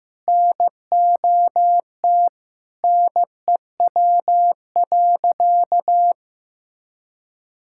Morse code, heard as a keyed tone, about 15 wpm, 700 Hz. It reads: NOT NEW.
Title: Sentences from Top 1600 Words 15wpm